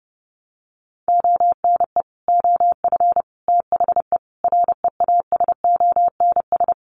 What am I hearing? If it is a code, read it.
Morse code, heard as a keyed tone, about 30 wpm, 700 Hz. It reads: ONE OF THE REASONS
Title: Sets of 4 Words 30wpm